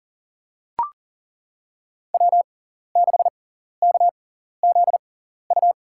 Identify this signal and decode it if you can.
Morse code, heard as a keyed tone, about 40 wpm, spaced as for 14 wpm, 700 Hz. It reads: W6KZU
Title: USA Call Signs - 5 Characters - 6x Character Spacing - 40wpm